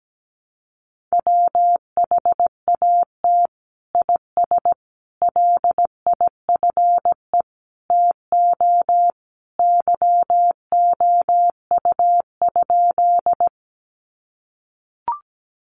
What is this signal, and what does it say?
Morse code, heard as a keyed tone, about 17 wpm, 700 Hz. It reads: WHAT IS LIFE TO YOU?
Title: Sentences from Top 300 Words 17wpm